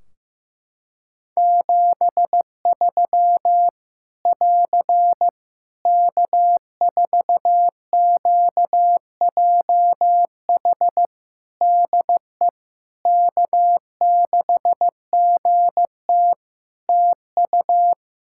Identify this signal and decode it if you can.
Morse code, heard as a keyed tone, about 15 wpm, 700 Hz. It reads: 73 <AR> K4QJH DE K6GT TU
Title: Practice QSO - Exchange 4 - Ending - 15wpm